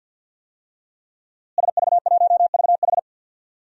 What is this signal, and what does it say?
Morse code, heard as a keyed tone, about 47 wpm, 700 Hz. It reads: SV1VH
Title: DXCC-100 Call Signs - 5 Characters - Repeated 2x - 50wpm